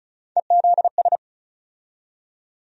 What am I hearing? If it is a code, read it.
Morse code, heard as a keyed tone, about 34 wpm, 700 Hz. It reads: EZS
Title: USA Call Sign Postfixes - 3 Letters 35wpm